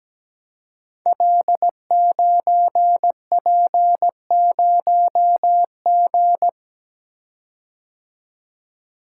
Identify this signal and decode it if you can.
Morse code, heard as a keyed tone, about 17 wpm, 700 Hz. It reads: L9P0G
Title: Code Groups - 5 Letter:Numbers - 17wpm